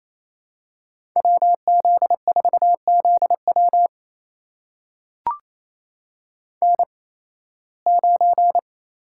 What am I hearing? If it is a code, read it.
Morse code, heard as a keyed tone, about 28 wpm, spaced as for 18 wpm, 700 Hz. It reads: WZ4ZW N9
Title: USA Call Signs - 5 Characters - 8x Character Spacing - 28wpm